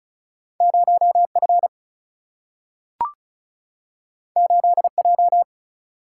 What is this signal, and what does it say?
Morse code, heard as a keyed tone, about 36 wpm, 700 Hz. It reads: 0F 8J
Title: Single-Letter Single-Number 35wpm